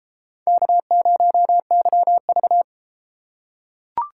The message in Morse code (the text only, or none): K0YV